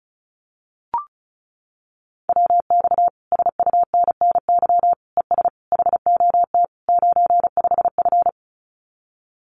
Morse code, heard as a keyed tone, about 36 wpm, 700 Hz. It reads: WX SUNNY ES HOT 95F
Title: Common QSO Phrases Encore 35wpm